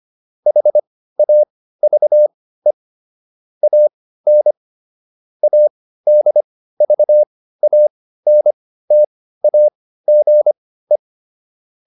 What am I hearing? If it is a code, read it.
Morse code, heard as a keyed tone, about 25 wpm, spaced as for 15 wpm, 600 Hz. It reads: HAVE AN ADVANTAGE